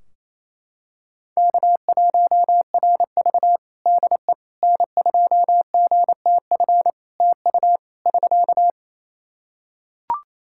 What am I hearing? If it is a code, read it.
Morse code, heard as a keyed tone, about 27 wpm, 700 Hz. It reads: K1RV DE N2GTF TU <SK>
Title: Practice QSO - Exchange 4 - Ending - 28wpm